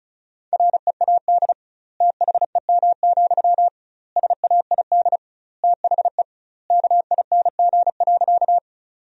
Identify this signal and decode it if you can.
Morse code, heard as a keyed tone, about 36 wpm, 700 Hz. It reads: READ THEM, SAID THE KING.
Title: 2x Word-Spacing - Sentences from Top 400 Words - 35wpm